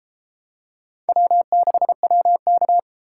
Morse code, heard as a keyed tone, about 33 wpm, 700 Hz. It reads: W6WK